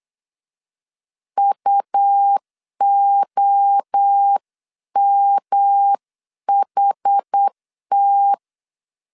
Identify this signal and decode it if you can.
Morse code, heard as a keyed tone, about 8 wpm, 800 Hz. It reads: UOMHT
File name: CW / DET / 1 3 1500C S C 8wpm 4214